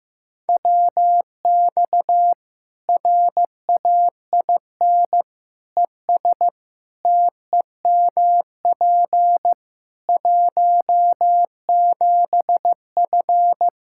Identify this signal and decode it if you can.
Morse code, heard as a keyed tone, about 15 wpm, 700 Hz. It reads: WX RAIN ES TEMP 17F
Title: Practice QSO - Exchange 2 - Info - 15wpm